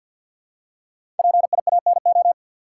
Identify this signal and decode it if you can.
Morse code, heard as a keyed tone, about 53 wpm, 700 Hz. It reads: PIANO
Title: Five Letter Words - 50wpm